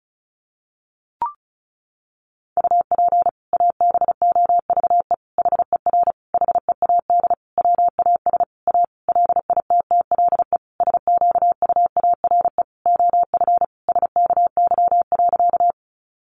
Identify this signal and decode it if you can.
Morse code, heard as a keyed tone, about 34 wpm, 700 Hz. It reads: UP ABOVE HER HEAD WAS A LITTLE SQUARE OF SKY.
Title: Sentences from Top 1200 Words 35wpm